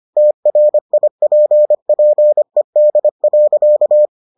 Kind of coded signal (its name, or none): Morse code